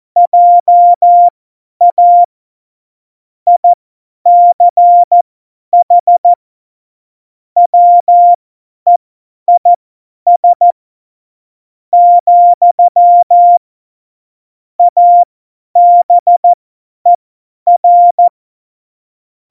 Morse code, heard as a keyed tone, about 14 wpm, spaced as for 10 wpm, 700 Hz. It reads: JA ICH WEIS , ABER